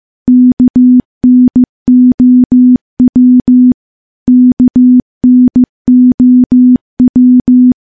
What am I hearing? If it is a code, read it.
Morse code, heard as a keyed tone, about 15 wpm, 250 Hz. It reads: KNOW KNOW
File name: morse / morse-training-15WPM